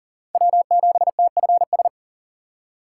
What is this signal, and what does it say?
Morse code, heard as a keyed tone, about 39 wpm, 700 Hz. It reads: W7TFS
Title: USA Call Signs - 2x Character Spacing - 40wpm